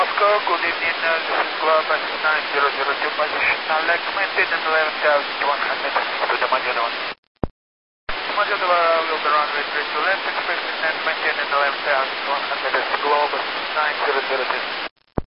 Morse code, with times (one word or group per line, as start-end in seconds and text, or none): none